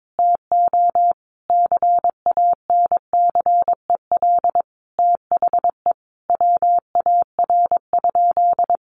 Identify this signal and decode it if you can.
Morse code, heard as a keyed tone, about 22 wpm, 700 Hz. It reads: TO CANCEL THE WAR?
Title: Sentences from Top 1200 Words 22wpm